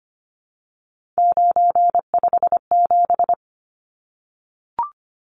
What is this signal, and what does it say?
Morse code, heard as a keyed tone, about 25 wpm, 700 Hz. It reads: 957
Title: Numbers - 3-Digit 25wpm